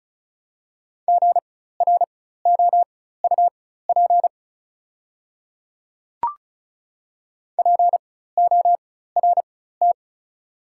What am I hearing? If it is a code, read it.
Morse code, heard as a keyed tone, about 36 wpm, spaced as for 16 wpm, 700 Hz. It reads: GROUP PORT
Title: 4x Character Spacing - Top 300 Words - 35wpm